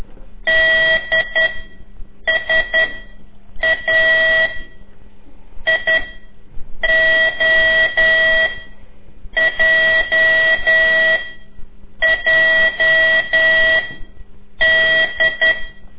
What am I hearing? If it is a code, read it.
Morse code, heard as a keyed tone, about 10 wpm, 1950 Hz. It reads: D S A I O J J D
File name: CORSI CW